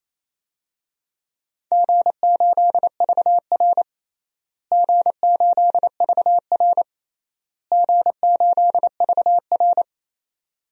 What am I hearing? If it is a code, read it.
Morse code, heard as a keyed tone, about 28 wpm, 700 Hz. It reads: G8VR G8VR G8VR
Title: DXCC-100 Call Signs - 4 Characters - Repeated 3x - 28wpm